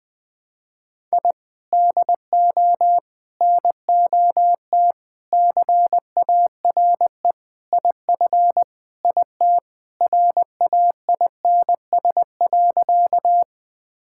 Morse code, heard as a keyed tone, about 20 wpm, 700 Hz. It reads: I DO NOT CARE IF IT RAINS.
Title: Sentences with 5-Word Phrases - 20wpm